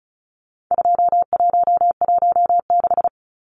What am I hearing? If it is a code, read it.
Morse code, heard as a keyed tone, about 34 wpm, 700 Hz. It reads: 2116
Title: Numbers - 4-Digit 35wpm